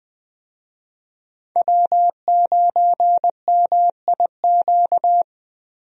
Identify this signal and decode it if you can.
Morse code, heard as a keyed tone, about 20 wpm, 700 Hz. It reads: W9MIQ
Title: USA Call Signs - 5 Characters - 20wpm